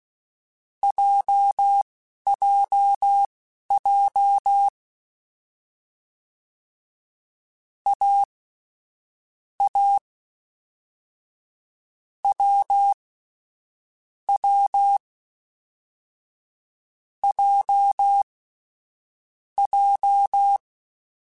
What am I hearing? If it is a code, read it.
Morse code, heard as a keyed tone, about 16 wpm, spaced as for 11 wpm, 800 Hz. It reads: JJJ A A W W J J
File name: Difficult Combos / Difficult Combos-mp3s / AWJ1 @10